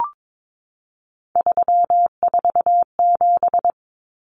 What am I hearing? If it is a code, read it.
Morse code, heard as a keyed tone, about 22 wpm, 700 Hz. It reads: 347
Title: Numbers - 3-Digit 22wpm